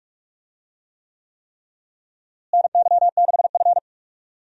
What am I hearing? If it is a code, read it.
Morse code, heard as a keyed tone, about 46 wpm, 700 Hz. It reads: NY6F
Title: USA Call Signs - 4 Characters - Repeated 3x - 45wpm